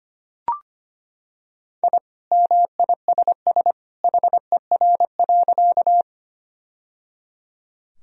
Morse code, heard as a keyed tone, about 25 wpm, 700 Hz. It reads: I MISS HER.
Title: Sentences from Top 1000 Words - Review Part 3 - 25wpm